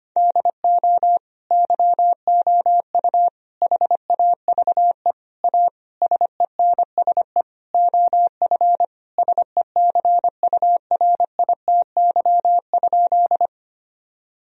Morse code, heard as a keyed tone, about 25 wpm, 700 Hz. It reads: DO YOU HAVE A SENSE OF SECURITY?
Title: Sentences from Top 1000 Words - Review Part 4 - 25wpm